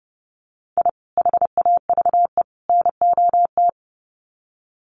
Morse code, heard as a keyed tone, about 30 wpm, 700 Hz. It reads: I HAVE NOT
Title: Sets of 3 Words 30wpm